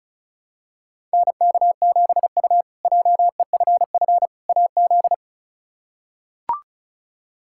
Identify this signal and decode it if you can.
Morse code, heard as a keyed tone, about 34 wpm, 700 Hz. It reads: NK7U JEFF AZ
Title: K1USN SST QSO Practice - 35wpm